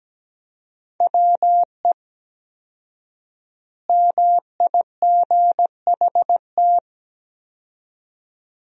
Morse code, heard as a keyed tone, about 17 wpm, 700 Hz. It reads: WE MIGHT
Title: 4x Word-Spacing - Sentences from Top 300 Words - 17wpm